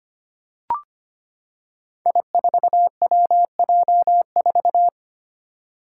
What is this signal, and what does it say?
Morse code, heard as a keyed tone, about 25 wpm, 700 Hz. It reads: I4WJ4
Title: Five Challenging Letter-Number Combinations 25wpm